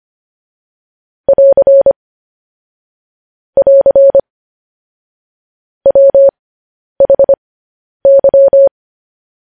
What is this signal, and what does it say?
Morse code, heard as a keyed tone, about 25 wpm, spaced as for 10 wpm, 550 Hz. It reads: <AR> <AR> WHY
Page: news headlines in morse code at 10 wpm